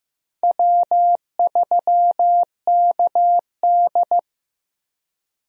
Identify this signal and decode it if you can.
Morse code, heard as a keyed tone, about 15 wpm, 700 Hz. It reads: W3KD
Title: USA Call Signs - 4 Characters - 15wpm